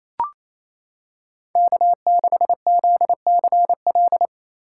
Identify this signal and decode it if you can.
Morse code, heard as a keyed tone, about 28 wpm, 700 Hz. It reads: K6ZCL
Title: K1USN SST QSO Practice - 28wpm